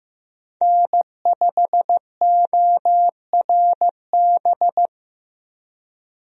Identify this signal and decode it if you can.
Morse code, heard as a keyed tone, about 15 wpm, 700 Hz. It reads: N5ORB
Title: USA Call Signs - 5 Characters - Repeated 2x - 15wpm